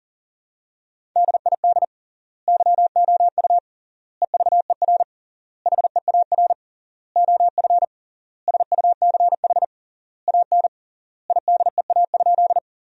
Morse code, heard as a keyed tone, about 40 wpm, 700 Hz. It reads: DID YOU EVER HEAR OF SUCH AN IDEA?
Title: 3x Word-Spacing - Sentences from Top 400 Words - 40wpm